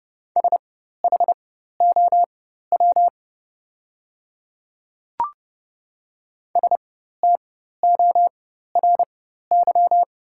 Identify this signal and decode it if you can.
Morse code, heard as a keyed tone, about 29 wpm, spaced as for 14 wpm, 690 Hz. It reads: SHOW STORY
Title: 4x Character Spacing - Top 500 Words - 30wpm